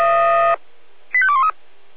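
Five-tone selective call (ZVEI), heard as quick runs of five tones, tones of about 70 ms. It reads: 85213
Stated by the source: ZVEI five-tone call received on SDR.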